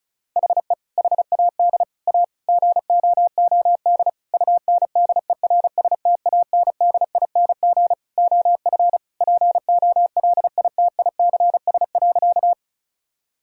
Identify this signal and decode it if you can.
Morse code, heard as a keyed tone, about 36 wpm, 700 Hz. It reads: HE HAD A GOOD UNDERSTANDING OF POLITICS.
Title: Sentences from Top 1000 Words - Review Part 6 - 35wpm